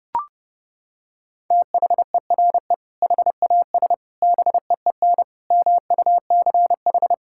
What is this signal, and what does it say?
Morse code, heard as a keyed tone, about 32 wpm, 700 Hz. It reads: THERE HAS BEEN MUCH
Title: Sentences from Top 1000 Words - Review Part 5 - 30wpm